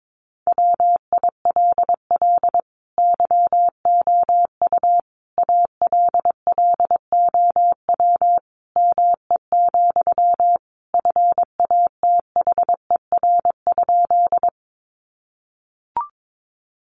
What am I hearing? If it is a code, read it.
Morse code, heard as a keyed tone, about 22 wpm, 700 Hz. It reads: WILL YOU ALLOW ME, FATHER?
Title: Sentences from Top 700 Words 22wpm